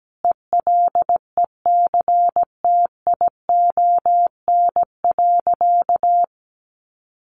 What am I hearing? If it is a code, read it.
Morse code, heard as a keyed tone, about 17 wpm, 700 Hz. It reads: ELECTION.
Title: Sentences from Top 1500 Words 17wpm